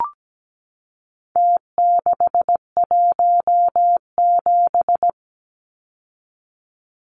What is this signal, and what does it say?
Morse code, heard as a keyed tone, about 17 wpm, 700 Hz. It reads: T617
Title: Numbers - Contest Style - 4-Digit 17wpm